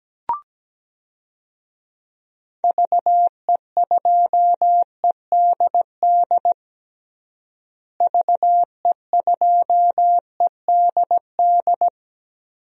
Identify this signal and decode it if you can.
Morse code, heard as a keyed tone, about 17 wpm, 700 Hz. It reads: VE2EDD VE2EDD
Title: DXCC-100 Call Signs - 6 Characters - Repeated 2x - 17wpm